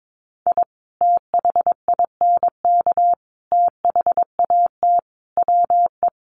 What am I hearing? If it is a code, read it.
Morse code, heard as a keyed tone, about 22 wpm, 700 Hz. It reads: I THINK THAT WE
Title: Sets of 5 Words Encore2 22wpm